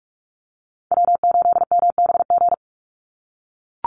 Morse code, heard as a keyed tone, about 44 wpm, 700 Hz. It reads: W8MBG
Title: USA Call Signs 45wpm